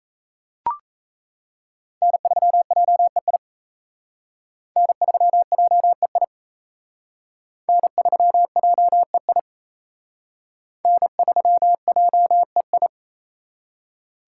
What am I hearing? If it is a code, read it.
Morse code, heard as a keyed tone, about 35 wpm, 700 Hz. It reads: N3JEI N3JEI N3JEI N3JEI